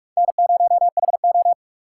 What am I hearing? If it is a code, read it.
Morse code, heard as a keyed tone, about 44 wpm, 700 Hz. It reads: N0HO